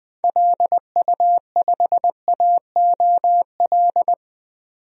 Morse code, heard as a keyed tone, about 20 wpm, 700 Hz. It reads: LU5AOL
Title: DXCC-100 Call Signs - 6 Characters - 20wpm